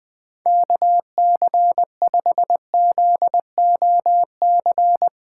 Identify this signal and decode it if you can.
Morse code, heard as a keyed tone, about 20 wpm, 700 Hz. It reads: KC5ZOC
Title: Speed Racing - USA Call Signs 20wpm